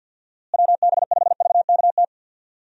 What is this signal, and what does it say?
Morse code, heard as a keyed tone, about 51 wpm, 700 Hz. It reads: WB5VKT
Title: USA Call Signs - 6 Characters - Repeated 3x - 50wpm